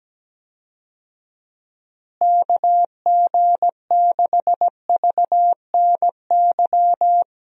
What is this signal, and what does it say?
Morse code, heard as a keyed tone, about 17 wpm, 700 Hz. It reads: KG6VNY
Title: USA Call Signs - 6 Characters - Repeated 3x - 17wpm